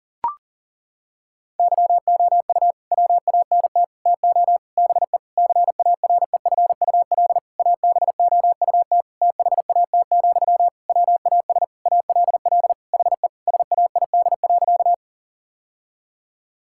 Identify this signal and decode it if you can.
Morse code, heard as a keyed tone, about 37 wpm, 700 Hz. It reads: YOU WANT TO BE CAREFUL ABOUT THAT, WAS ALL HE SAID.
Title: Sentences from Top 1600 Words 40wpm